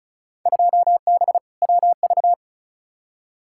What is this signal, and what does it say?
Morse code, heard as a keyed tone, about 36 wpm, 700 Hz. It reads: 2B WV
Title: Field Day Exchanges Encore - 35wpm